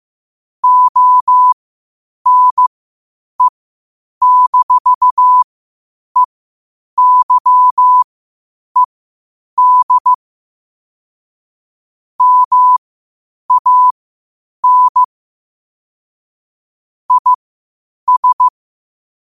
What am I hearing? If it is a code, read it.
Morse code, heard as a keyed tone, about 15 wpm, spaced as for 8 wpm, 1000 Hz. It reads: ONE-EYED MAN IS
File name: qotdfiles / 8wpm